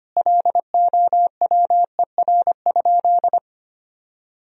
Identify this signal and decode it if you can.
Morse code, heard as a keyed tone, about 25 wpm, 700 Hz. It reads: LOWER?
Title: Sentences from Top 1000 Words - Review Part 4 - 25wpm